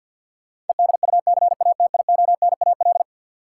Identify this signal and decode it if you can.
Morse code, heard as a keyed tone, about 47 wpm, 700 Hz. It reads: EDUCATIONAL
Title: Sentences from Top 1200 Words - New Word Drill Alt - 50wpm